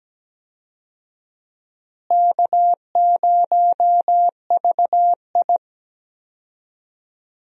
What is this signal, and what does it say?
Morse code, heard as a keyed tone, about 17 wpm, 700 Hz. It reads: K0VI
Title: USA Call Signs - 4 Characters - Repeated 3x - 17wpm